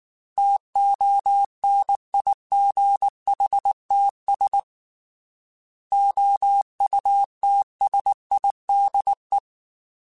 Morse code, heard as a keyed tone, about 19 wpm, 800 Hz. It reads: TONIGHTS OUTSIDE
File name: Weather Sets / Weather Sets-mp3s / Weather 5 1 @18